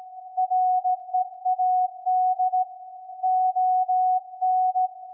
Morse code, heard as a keyed tone, about 16 wpm, 740 Hz. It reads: READ ON
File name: iz7slz / VLF / SAQ 24-12-2021